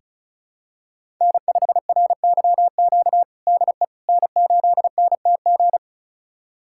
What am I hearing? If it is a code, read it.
Morse code, heard as a keyed tone, about 36 wpm, 700 Hz. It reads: N5RYQ DE N8NTG
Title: USA Callsign DE Callsign - 5 Characters - 35wpm